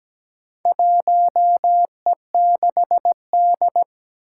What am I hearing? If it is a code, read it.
Morse code, heard as a keyed tone, about 17 wpm, 700 Hz. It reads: 1E6D
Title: Four Challenging Letter-Number Combinations 17wpm